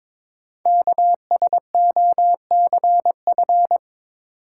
Morse code, heard as a keyed tone, about 22 wpm, 700 Hz. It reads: KSOCF
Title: Code Groups - 5 Letter:Numbers - 22wpm